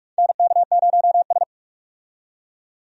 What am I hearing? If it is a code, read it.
Morse code, heard as a keyed tone, about 44 wpm, 700 Hz. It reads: NK0S
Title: USA Call Signs - 4 Characters - 45wpm